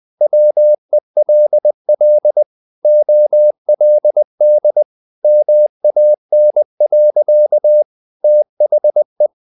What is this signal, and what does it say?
Morse code, heard as a keyed tone, about 20 wpm, 600 Hz. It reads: WELL OLD MAN. THE
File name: the old man and the sea / the old man and the sea part 1 20wpm